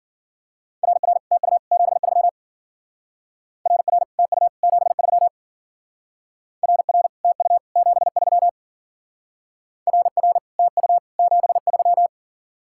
Morse code, heard as a keyed tone, about 52 wpm, 700 Hz. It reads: RR TU 73 RR TU 73 RR TU 73 RR TU 73